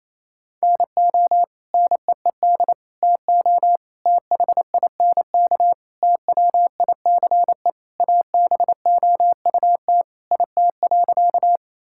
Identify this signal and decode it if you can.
Morse code, heard as a keyed tone, about 27 wpm, 700 Hz. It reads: NO NEED TO THINK TWICE ABOUT IT.
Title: Sentences from Top 1600 Words 28wpm